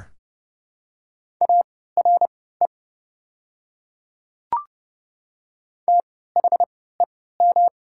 Morse code, heard as a keyed tone, about 30 wpm, spaced as for 17 wpm, 690 Hz. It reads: ARE THEM
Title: 3x Character Spacing - Top 100 Words - 30wpm